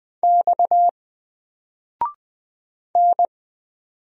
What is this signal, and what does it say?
Morse code, heard as a keyed tone, about 20 wpm, 700 Hz. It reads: X N
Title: Single Letter-Number 20wpm